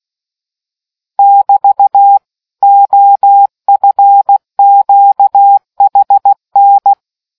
Morse code, heard as a keyed tone, about 16 wpm, 800 Hz. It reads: <BT> OFQHN